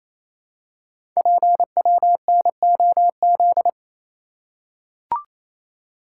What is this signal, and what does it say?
Morse code, heard as a keyed tone, about 28 wpm, 700 Hz. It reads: PWNOZ
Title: Code Groups - 5 Letters - 28wpm